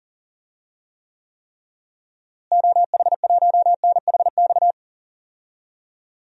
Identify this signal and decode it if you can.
Morse code, heard as a keyed tone, about 40 wpm, 700 Hz. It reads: OH1NHX
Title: DXCC-100 Call Signs - 6 Characters - Repeated 3x - 40wpm